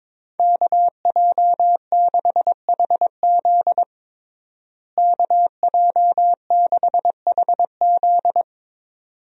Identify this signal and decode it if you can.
Morse code, heard as a keyed tone, about 22 wpm, 700 Hz. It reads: KJ6HZ KJ6HZ
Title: USA Call Signs - 5 Characters - Repeated 2x - 22wpm